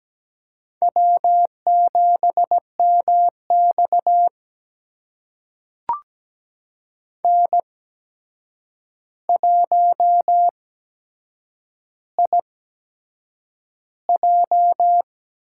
Morse code, heard as a keyed tone, about 17 wpm, spaced as for 7 wpm, 700 Hz. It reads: W7MX N1IJ
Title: USA Call Signs - 4 Characters - 8x Character Spacing - 17wpm